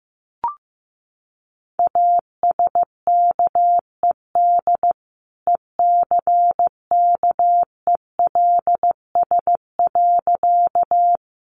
Morse code, heard as a keyed tone, about 15 wpm, 700 Hz. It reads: ASKED ECKELS.
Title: A Sound Of Thunder 15wpm